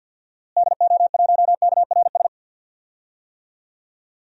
Morse code, heard as a keyed tone, about 52 wpm, 700 Hz. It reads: DO1KRS
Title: DXCC-100 Call Signs - 6 Characters - Repeated 3x - 50wpm